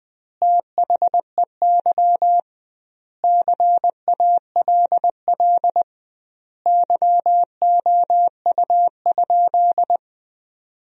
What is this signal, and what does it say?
Morse code, heard as a keyed tone, about 21 wpm, 700 Hz. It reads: THEY CALL YOU?
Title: 2x Word-Spacing - Sentences from Top 100 Words - 20wpm